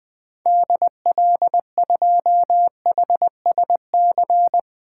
Morse code, heard as a keyed tone, about 20 wpm, 700 Hz. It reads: DL2HSC
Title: 2023 SOTA Activator Call Signs - 20wpm